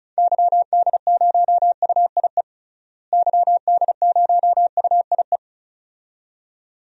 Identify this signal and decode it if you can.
Morse code, heard as a keyed tone, about 36 wpm, 700 Hz. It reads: YD0UIE YD0UIE
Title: DXCC-100 Call Signs - 6 Characters - Repeated 2x - 35wpm